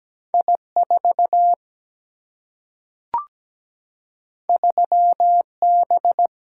Morse code, heard as a keyed tone, about 17 wpm, 700 Hz. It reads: I 4 3 B
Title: Single-Letter Single-Number 17wpm